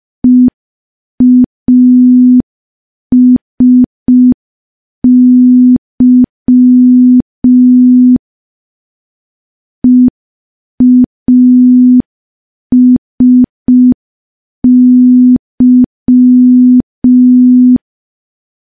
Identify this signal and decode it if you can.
Morse code, heard as a keyed tone, about 5 wpm, 250 Hz. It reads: EASY EASY
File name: morse / morse-training-05WPM